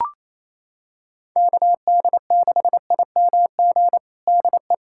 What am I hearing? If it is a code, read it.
Morse code, heard as a keyed tone, about 28 wpm, 700 Hz. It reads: KD6IMG DE
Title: USA Callsign DE Callsign 28wpm